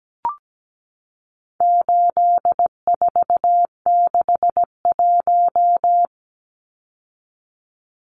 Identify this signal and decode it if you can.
Morse code, heard as a keyed tone, about 17 wpm, 700 Hz. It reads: 8461
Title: Numbers - 4-Digit 17wpm